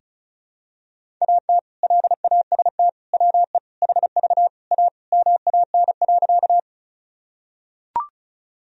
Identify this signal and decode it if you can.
Morse code, heard as a keyed tone, about 36 wpm, 700 Hz. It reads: AT LAST WE HV A MAN.
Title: Sentences from Top 300 Words and Abbreviations 35wpm